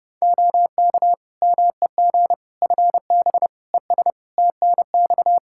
Morse code, heard as a keyed tone, about 30 wpm, 700 Hz. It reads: OK MEG FB ES TNX
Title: Practice QSO - Exchange 2 - Info - 30wpm